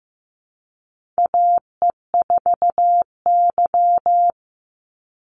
Morse code, heard as a keyed tone, about 15 wpm, 700 Hz. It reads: AE4Y
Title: NA SOTA Chaser Callsigns 15wpm